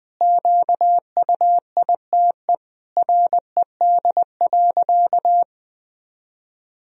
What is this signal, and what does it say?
Morse code, heard as a keyed tone, about 20 wpm, 700 Hz. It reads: QUITE RED.
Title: Sentences from Top 1000 Words - Review Part 2 - 20wpm